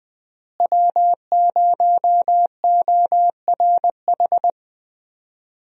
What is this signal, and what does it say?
Morse code, heard as a keyed tone, about 20 wpm, 700 Hz. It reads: W0ORH
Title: USA Call Signs - 5 Characters - Repeated 3x - 20wpm